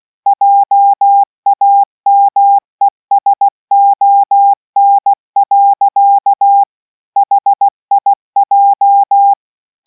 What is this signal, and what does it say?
Morse code, heard as a keyed tone, about 16 wpm, 800 Hz. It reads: JAMESON. HIJ